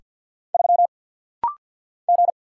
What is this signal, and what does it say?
Morse code, heard as a keyed tone, about 48 wpm, 690 Hz. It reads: 3 G
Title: Single Letter-Number - Rapid-Fire 50wpm